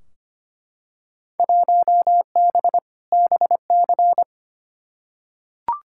Morse code, heard as a keyed tone, about 25 wpm, 700 Hz. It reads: 1B BC
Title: Field Day Exchanges - 25wpm